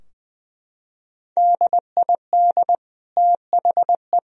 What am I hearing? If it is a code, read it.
Morse code, heard as a keyed tone, about 20 wpm, 700 Hz. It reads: DID THE
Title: Sentences from Top 1000 Words - Review Part 2 - 20wpm